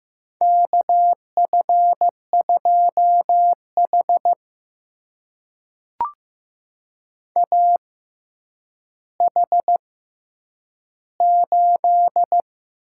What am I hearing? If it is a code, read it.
Morse code, heard as a keyed tone, about 15 wpm, spaced as for 9 wpm, 700 Hz. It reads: KF2H AH8
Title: USA Call Signs - 4 Characters - 6x Character Spacing - 15wpm